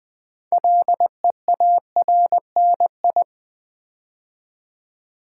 Morse code, heard as a keyed tone, about 20 wpm, 700 Hz. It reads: LEARNI
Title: Build A Word - 3 Letters At A Time - 8-Letter Words - 20wpm